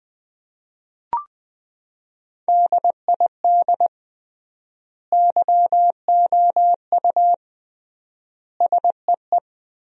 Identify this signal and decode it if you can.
Morse code, heard as a keyed tone, about 20 wpm, 700 Hz. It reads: DID YOU SEE
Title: 3x Word-Spacing - Sentences from Top 100 Words - 20wpm